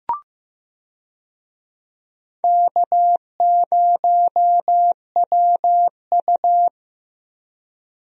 Morse code, heard as a keyed tone, about 15 wpm, 700 Hz. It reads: K0WU
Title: USA Call Signs - 4 Characters - Repeated 2x - 15wpm